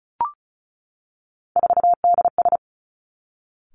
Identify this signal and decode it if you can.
Morse code, heard as a keyed tone, about 36 wpm, 700 Hz. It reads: 4DS
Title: Three Challenging Letter-Number Combinations 35wpm